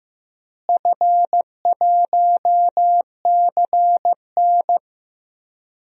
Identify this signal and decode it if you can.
Morse code, heard as a keyed tone, about 15 wpm, 700 Hz. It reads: F1CN
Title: DXCC-100 Call Signs - 4 Characters - Repeated 3x - 15wpm